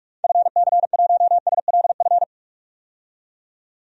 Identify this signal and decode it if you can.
Morse code, heard as a keyed tone, about 46 wpm, 700 Hz. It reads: FC1SLF
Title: DXCC-100 Call Signs - 6 Characters - 45wpm